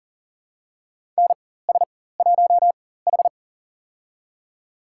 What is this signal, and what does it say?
Morse code, heard as a keyed tone, about 37 wpm, spaced as for 18 wpm, 700 Hz. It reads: NS1H
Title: USA Call Signs - 4 Characters - 4x Character Spacing - 40wpm